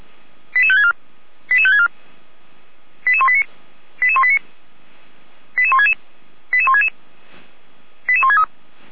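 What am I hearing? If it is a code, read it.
Five-tone selective call (ZVEI), heard as quick runs of five tones, tones of about 70 ms. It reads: 88564 88564 88189 88189 88166 88166 88163
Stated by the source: ZVEI five-tone selective call received on SDR.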